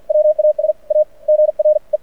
Morse code, heard as a keyed tone, about 50 wpm, spaced as for 44 wpm, 600 Hz. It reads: WAS A GRE
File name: archived / CRSnet